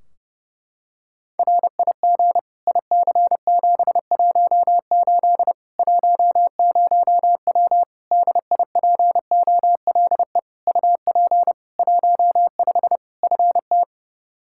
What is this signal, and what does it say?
Morse code, heard as a keyed tone, about 30 wpm, 700 Hz. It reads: RIG IC718 10W DIPOLE UP 15 FT
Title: Practice QSO - Exchange 2 - Info - 30wpm